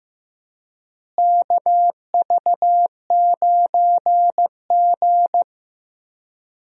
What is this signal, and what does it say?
Morse code, heard as a keyed tone, about 15 wpm, 700 Hz. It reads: KV9G
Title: USA Call Signs - 4 Characters - Repeated 3x - 15wpm